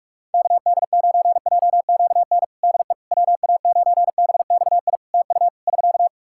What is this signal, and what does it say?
Morse code, heard as a keyed tone, about 46 wpm, 700 Hz. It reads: KD9JQN DE WA9BXI TU <SK>
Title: Practice QSO - Exchange 4 - Ending - 45wpm